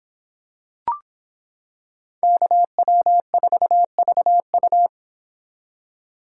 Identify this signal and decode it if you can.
Morse code, heard as a keyed tone, about 26 wpm, 700 Hz. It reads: KW4VU